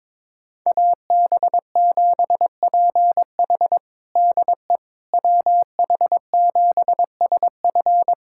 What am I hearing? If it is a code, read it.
Morse code, heard as a keyed tone, about 22 wpm, 700 Hz. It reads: AB7PH DE WH7SF
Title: USA Callsign DE Callsign - 5 Characters - 22wpm